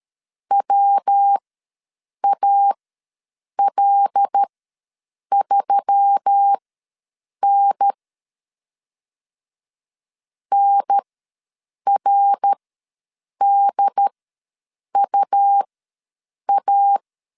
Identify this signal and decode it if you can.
Morse code, heard as a keyed tone, about 13 wpm, spaced as for 7 wpm, 800 Hz. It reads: WAL3N NRDUA
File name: LEZIONI / 5 LBVC / 5 L ws 12wpm 2958